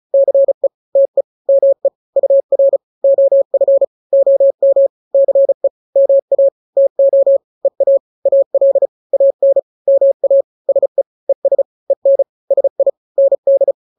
Morse code, heard as a keyed tone, about 34 wpm, 550 Hz. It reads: CE TE ME UR OF OM CE MA TO EA AL AN MA SE ES EN SI ND
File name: n-grams / 2-grams 35 30 WPM